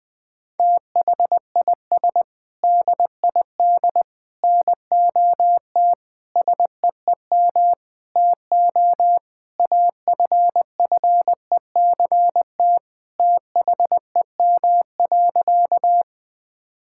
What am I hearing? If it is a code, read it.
Morse code, heard as a keyed tone, about 20 wpm, 700 Hz. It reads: THIS DID NOT SEEM TO AFFECT THEM.
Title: Sentences from Top 1000 Words - Review Part 6 - 20wpm